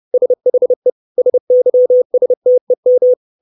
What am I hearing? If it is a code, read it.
Morse code, heard as a keyed tone, about 30 wpm, 500 Hz. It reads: SHE SYSTEM
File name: 632 Words 30WPM